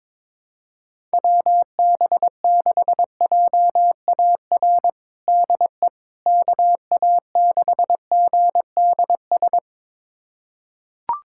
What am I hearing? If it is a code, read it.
Morse code, heard as a keyed tone, about 22 wpm, 700 Hz. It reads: WB6JAR DE KA6GDS
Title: USA Callsign DE Callsign - 6 Characters - 22wpm